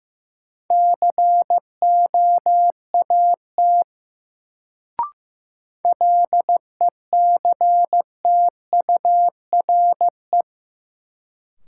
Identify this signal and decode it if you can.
Morse code, heard as a keyed tone, about 15 wpm, 700 Hz. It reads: COAT LECTURE
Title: Sentences from Top 1600 Words - New Word Drill - 15wpm